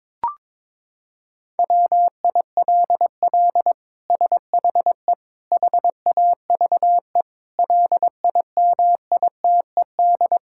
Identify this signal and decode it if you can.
Morse code, heard as a keyed tone, about 22 wpm, 700 Hz. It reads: WILL SHE HAVE LIMITED
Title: Sentences from Top 1000 Words - Review Part 6 - 22wpm